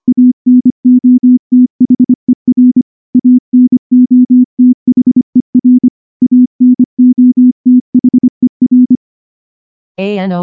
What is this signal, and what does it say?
Morse code, heard as a keyed tone, about 24 wpm, 250 Hz. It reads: ANOTHER ANOTHER ANOTHER
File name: morse / morse-training-25WPM